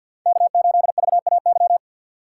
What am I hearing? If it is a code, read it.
Morse code, heard as a keyed tone, about 52 wpm, 700 Hz. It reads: K8VAY
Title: USA Call Signs - 5 Characters - 8x Character Spacing - 50wpm